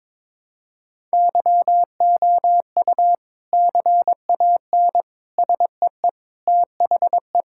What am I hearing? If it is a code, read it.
Morse code, heard as a keyed tone, about 22 wpm, 700 Hz. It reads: YOU CAN SEE THE
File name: practice / 22wpm / Sentences with 3-Word Phrases - 22wpm